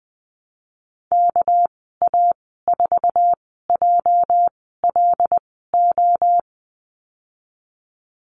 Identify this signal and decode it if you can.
Morse code, heard as a keyed tone, about 20 wpm, spaced as for 15 wpm, 700 Hz. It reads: KA4JLO
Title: 231 Lesson 37g - Letter x - Call Signs 20wpm